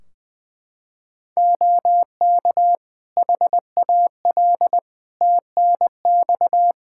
Morse code, heard as a keyed tone, about 20 wpm, 700 Hz. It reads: OK HAL TNX
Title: Practice QSO - Exchange 4 - Ending - 20wpm